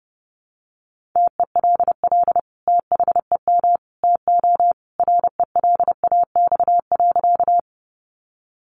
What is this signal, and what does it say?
Morse code, heard as a keyed tone, about 30 wpm, 700 Hz. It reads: TELL THEM TO RELAX.
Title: Sentences from Top 1400 Words 30wpm